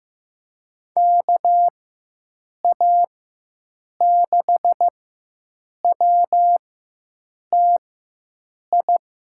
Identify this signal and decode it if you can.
Morse code, heard as a keyed tone, about 15 wpm, spaced as for 7 wpm, 700 Hz. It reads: KA6WTI
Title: USA Call Signs - 6 Characters - 4x Character Spacing - 15wpm